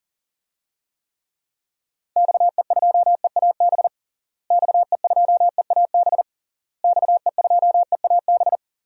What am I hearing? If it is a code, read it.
Morse code, heard as a keyed tone, about 40 wpm, 700 Hz. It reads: XE2EAB XE2EAB XE2EAB
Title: DXCC-100 Call Signs - 6 Characters - Repeated 3x - 40wpm